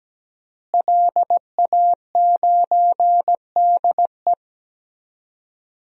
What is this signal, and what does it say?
Morse code, heard as a keyed tone, about 17 wpm, 700 Hz. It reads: LA9DE
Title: DXCC-100 Call Signs - 5 Characters - Repeated 2x - 17wpm